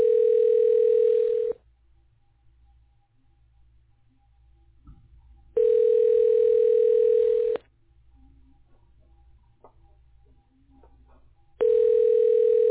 Ringback tone (North America), 440 + 480 Hz.